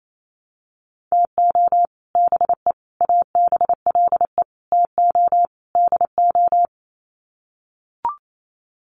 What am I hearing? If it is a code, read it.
Morse code, heard as a keyed tone, about 28 wpm, 700 Hz. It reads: TO BE ABLE TO DO
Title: Sets of 5 Words 28wpm